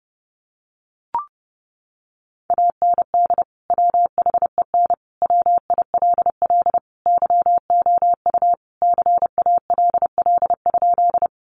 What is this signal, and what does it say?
Morse code, heard as a keyed tone, about 30 wpm, 700 Hz. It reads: AND WHEN WILL YOU CALL?
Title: Sentences from Top 100 Words 30wpm